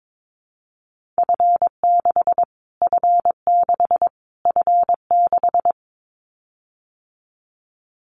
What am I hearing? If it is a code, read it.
Morse code, heard as a keyed tone, about 22 wpm, 700 Hz. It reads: F6 F6 F6
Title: Callsign Prefixes for Top 25 DXCC - Repeated 3x - 22wpm